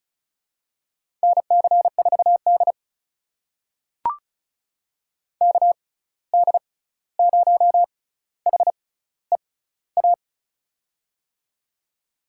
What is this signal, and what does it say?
Morse code, heard as a keyed tone, about 36 wpm, spaced as for 16 wpm, 700 Hz. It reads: NC4D KD0HEA
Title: USA Call Signs - 6x Character Spacing - 35wpm